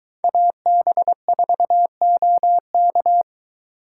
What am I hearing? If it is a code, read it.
Morse code, heard as a keyed tone, about 21 wpm, 700 Hz. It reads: AB4OK